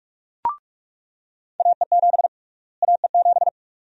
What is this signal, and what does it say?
Morse code, heard as a keyed tone, about 46 wpm, 700 Hz. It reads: AE7 AE7
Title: USA Call Sign Prefixes - Repeated 2x - 45wpm